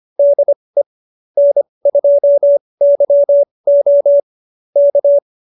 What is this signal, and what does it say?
Morse code, heard as a keyed tone, about 25 wpm, spaced as for 20 wpm, 590 Hz. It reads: DE N2YO K
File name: QSO316 20